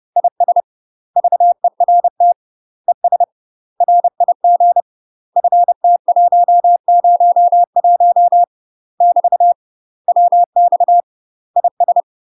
Morse code, heard as a keyed tone, about 30 wpm, 700 Hz. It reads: IS VERT ES RIG FT101 <BT> WX IS
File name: UK MASTER QSOs TRAINING / 30WPM / UK MASTER QSOs 0001